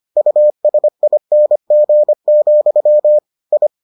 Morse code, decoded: USING, I

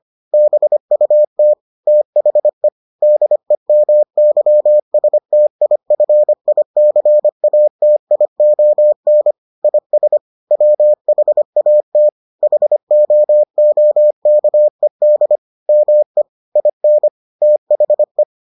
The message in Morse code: BUT THE DEMYSTIFICATION IS WHAT HOOKED ME IN THE

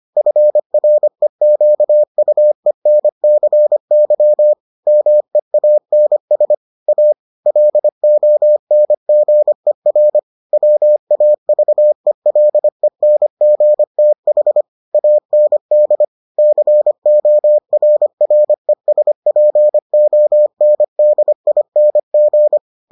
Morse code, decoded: FREQUENCY MEANS A LONGER WAVELENGTH AND CORRESPONDING

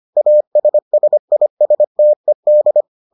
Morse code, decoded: ASSISTED

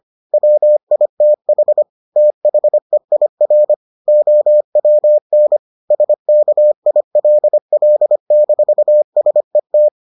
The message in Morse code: WITH THEIR OWN SKILL-SET

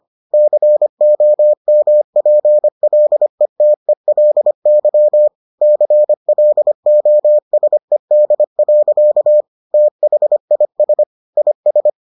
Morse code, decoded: COMPLETELY CLOSED. THIS IS

